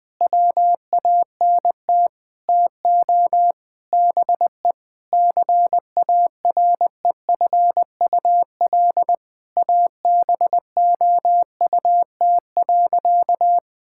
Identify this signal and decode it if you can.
Morse code, heard as a keyed tone, about 21 wpm, 700 Hz. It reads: WANT TO BE CAREFUL ABOUT.
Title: Sentences from Top 1600 Words 20wpm